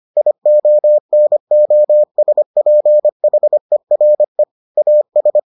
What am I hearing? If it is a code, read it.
Morse code, heard as a keyed tone, about 25 wpm, 600 Hz. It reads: IONOSPHERE AS